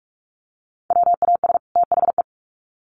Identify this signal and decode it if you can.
Morse code, heard as a keyed tone, about 44 wpm, 700 Hz. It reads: WAS THE
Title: Sets of 2 Words 45wpm